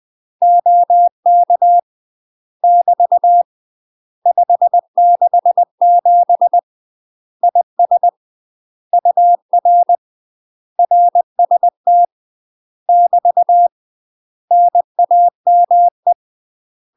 Morse code, heard as a keyed tone, about 20 wpm, 700 Hz. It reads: OK <BT> 567 IS UR RST <BT> NAME
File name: UK MASTER QSOs TRAINING / 20WPM / UK MASTER QSOs 0013